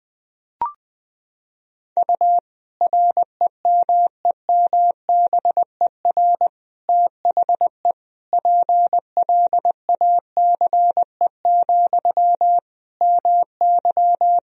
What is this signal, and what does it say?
Morse code, heard as a keyed tone, about 20 wpm, 700 Hz. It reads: U REMEMBER THE PLACE, MY
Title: Sentences from Top 400 Words and Abbreviations 20wpm